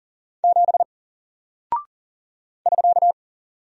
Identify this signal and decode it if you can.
Morse code, heard as a keyed tone, about 39 wpm, 700 Hz. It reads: 7 <SK>
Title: Single Character (Letter:Number:Punctuation:Prosign) Alt 40wpm